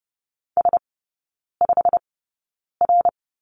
Morse code, heard as a keyed tone, about 30 wpm, spaced as for 9 wpm, 700 Hz. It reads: S5R
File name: course / 30:10wpm / 087 Lesson 15e - Number 5 - Three Characters 30wpm